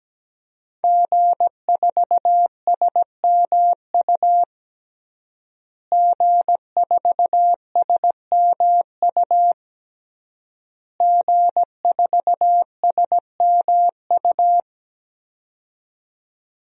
Morse code, heard as a keyed tone, about 17 wpm, 700 Hz. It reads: G4SMU G4SMU G4SMU
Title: DXCC-100 Call Signs - 5 Characters - Repeated 3x - 17wpm